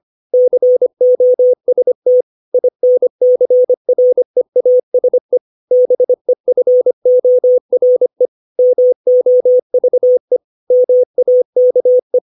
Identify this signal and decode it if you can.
Morse code, heard as a keyed tone, about 25 wpm, 500 Hz. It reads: COST INCREASE BEFORE MOVE MAKE
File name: 632 words / 632 Words 25WPM